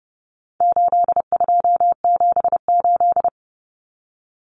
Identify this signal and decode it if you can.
Morse code, heard as a keyed tone, about 30 wpm, 700 Hz. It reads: 8278
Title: Numbers - 4-Digit 30wpm